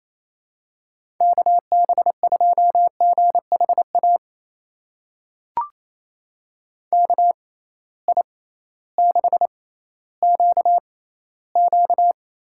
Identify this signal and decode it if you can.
Morse code, heard as a keyed tone, about 28 wpm, spaced as for 15 wpm, 700 Hz. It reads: KB2GHA KI6QQ
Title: USA Call Signs - 6 Characters - 6x Character Spacing - 28wpm